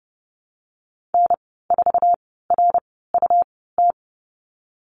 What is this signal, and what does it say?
Morse code, heard as a keyed tone, about 30 wpm, 700 Hz. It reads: N 4 R U T
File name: Lesson 19g - Letter w - Call Signs 30wpm